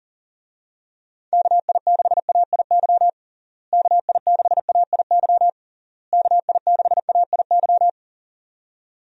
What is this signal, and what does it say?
Morse code, heard as a keyed tone, about 40 wpm, 700 Hz. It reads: KI6AIY KI6AIY KI6AIY
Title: USA Call Signs - 6 Characters - Repeated 3x - 40wpm